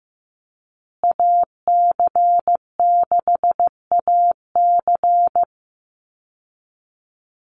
Morse code, heard as a keyed tone, about 15 wpm, 700 Hz. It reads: AC6AC
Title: Call Signs Repeated 3x - 15wpm